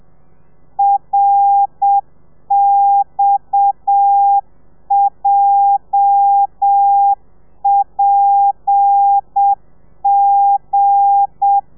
Morse code, heard as a keyed tone, about 7 wpm, 800 Hz. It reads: RXJPG
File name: telegrafia / i2tfj / Lez8 35